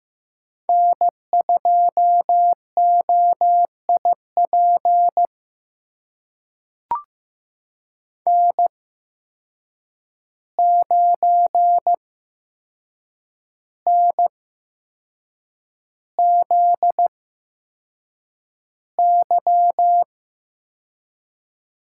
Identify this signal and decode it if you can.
Morse code, heard as a keyed tone, about 15 wpm, spaced as for 6 wpm, 700 Hz. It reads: N2OIP N9NZY
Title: USA Call Signs - 5 Characters - 8x Character Spacing - 15wpm